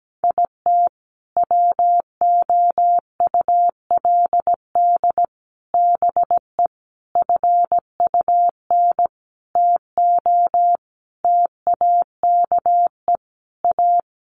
Morse code, heard as a keyed tone, about 17 wpm, 700 Hz. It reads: IT WOULD BE FUN TO TAKE A
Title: Sentences from Top 1300 Words 17wpm